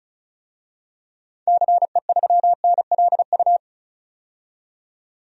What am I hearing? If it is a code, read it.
Morse code, heard as a keyed tone, about 34 wpm, 700 Hz. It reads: CE3NLU
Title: DXCC-100 Call Signs - 6 Characters - 35wpm